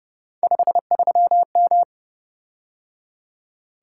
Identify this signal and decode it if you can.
Morse code, heard as a keyed tone, about 29 wpm, 700 Hz. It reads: 53M
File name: Speed-Racing - Three Letter-Number Combinations - 25wpm